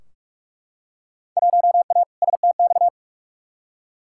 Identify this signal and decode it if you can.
Morse code, heard as a keyed tone, about 44 wpm, 700 Hz. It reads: 1A STX
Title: Field Day Exchanges - 45wpm